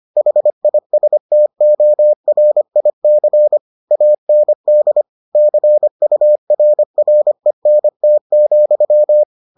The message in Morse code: HISTORIC AND CURRENT,

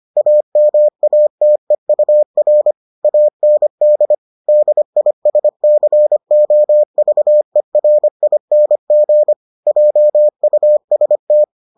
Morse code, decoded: AMATEUR AND DISCOVERING JUST

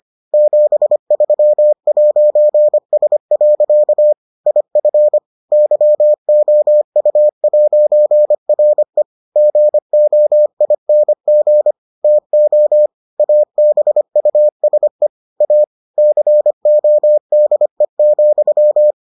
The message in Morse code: 73'S. IF YOU'RE GOING TO ABUSE A CODE,

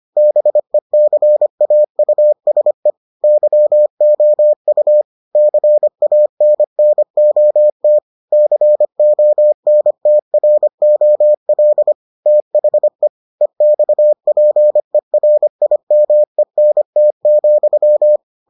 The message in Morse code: BECAUSE YOU CANNOT CONTROL THE EXPERIMENT,

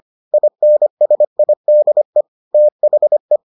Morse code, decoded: INSIDE THE

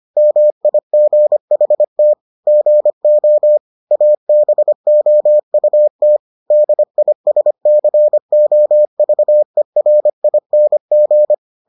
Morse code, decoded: MIGHT GO ABOUT DISCOVERING

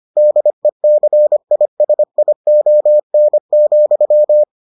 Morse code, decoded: DECISION,

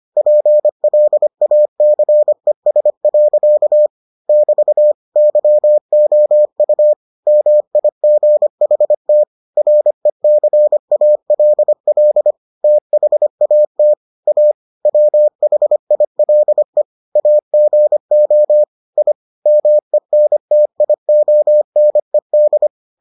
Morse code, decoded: PLACES. <BT> YOU MIGHT RECALL THAT A WHILE AGO I MENTIONED

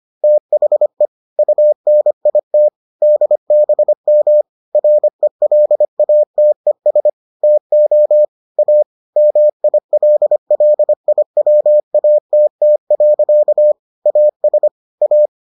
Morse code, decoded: THE UNIT DBM RELATES TO A MILLIWATT. AS A

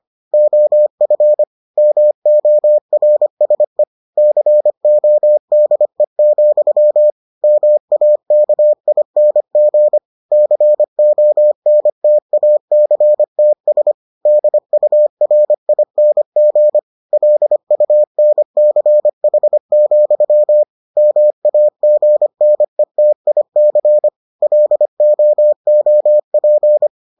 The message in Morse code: OF MORSE CODE, MAKING CONTACTS DURING LUNCH, MAGNETIC LOOP

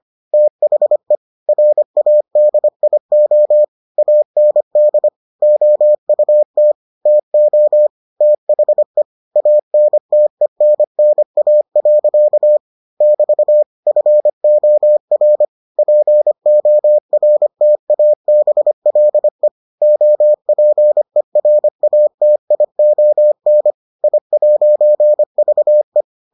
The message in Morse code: THE RADIO AND OUT TO THE ANTENNA. <BT> FOR PORTABLE OPERATION I'VE

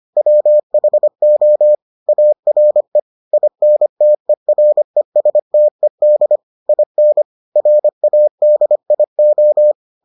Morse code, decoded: WHO ARE INTERESTED IN RADIO

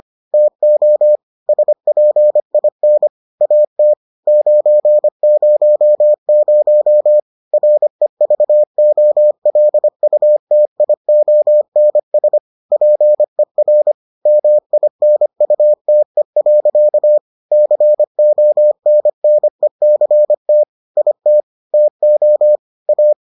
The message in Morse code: TO SPIN AT 900 REVOLUTIONS PER MINUTE. CONNECT IT TO A

